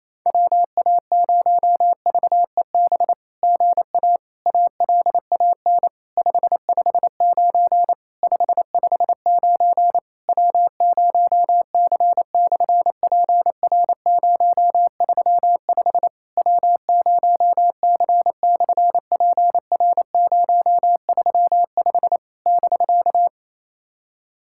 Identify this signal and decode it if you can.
Morse code, heard as a keyed tone, about 27 wpm, 700 Hz. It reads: WA0VEB GA ALAN 559 559 W0C/PR035 W0C/PR035 <BK>
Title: SOTA QSO Practice 28wpm